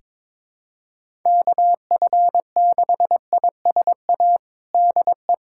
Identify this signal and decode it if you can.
Morse code, heard as a keyed tone, about 22 wpm, 700 Hz. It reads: KF6ISA DE
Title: USA Callsign DE Callsign - 6 Characters - 22wpm